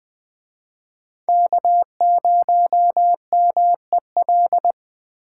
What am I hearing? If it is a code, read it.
Morse code, heard as a keyed tone, about 21 wpm, 700 Hz. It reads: K0MEL